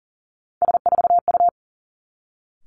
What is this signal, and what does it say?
Morse code, heard as a keyed tone, about 41 wpm, 700 Hz. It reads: S4U
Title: Three Challenging Letter-Number Combinations 40wpm